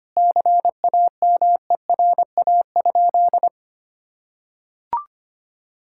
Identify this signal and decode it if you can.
Morse code, heard as a keyed tone, about 25 wpm, 700 Hz. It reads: CAMERA?
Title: Sentences from Top 1000 Words - Review Part 5 - 25wpm